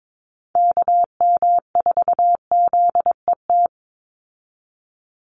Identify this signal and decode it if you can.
Morse code, heard as a keyed tone, about 22 wpm, 700 Hz. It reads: KM4ZET
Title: USA Call Signs 22wpm